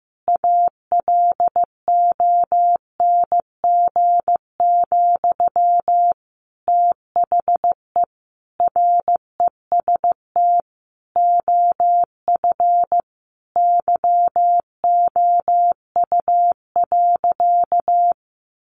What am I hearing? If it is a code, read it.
Morse code, heard as a keyed tone, about 15 wpm, 700 Hz. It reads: ALONG, THE REST OF YOU.
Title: Sentences from Top 700 Words 15wpm